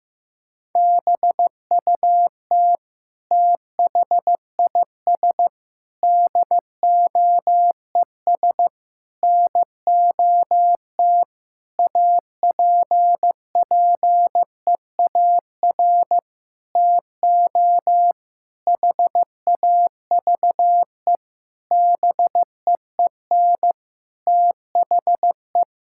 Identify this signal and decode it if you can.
Morse code, heard as a keyed tone, about 15 wpm, 700 Hz. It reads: BUT THIS DOES NOT APPEAR TO HAVE BEEN THE
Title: Sentences from Top 1000 Words - Review Part 6 - 15wpm